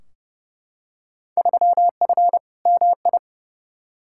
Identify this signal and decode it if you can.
Morse code, heard as a keyed tone, about 30 wpm, 700 Hz. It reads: 3F MI
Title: Field Day Exchanges - 30wpm